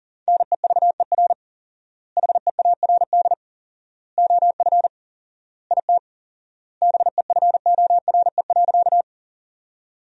Morse code, heard as a keyed tone, about 40 wpm, 700 Hz. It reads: NEVER HEARD OF IT BEFORE.
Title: 4x Word-Spacing - Sentences from Top 400 Words - 40wpm